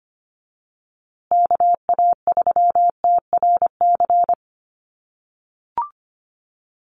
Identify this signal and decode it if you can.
Morse code, heard as a keyed tone, about 25 wpm, 700 Hz. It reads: KA3TRC
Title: USA Call Signs Repeated 3x 25wpm